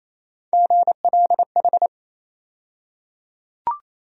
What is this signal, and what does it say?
Morse code, heard as a keyed tone, about 27 wpm, 700 Hz. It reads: GLH
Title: USA Call Sign Postfixes - 3 Letters 28wpm